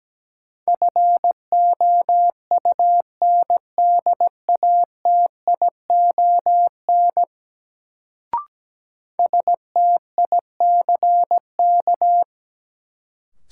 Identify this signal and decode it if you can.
Morse code, heard as a keyed tone, about 17 wpm, 700 Hz. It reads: FOUNDATION STICK
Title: Sentences from Top 1200 Words - New Word Drill Alt - 17wpm